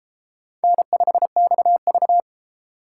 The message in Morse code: N5XV

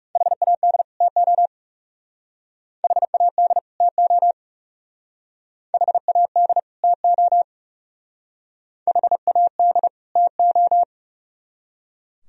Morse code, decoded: HAD TO HAD TO HAD TO HAD TO

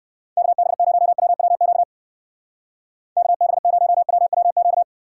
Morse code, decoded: KB0WWQ KB0WWQ

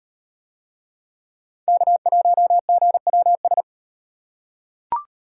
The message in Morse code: K1GWS

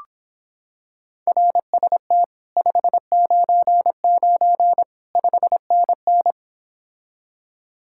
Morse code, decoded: RST 599 5NN

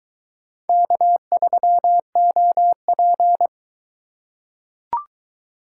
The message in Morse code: K3OP

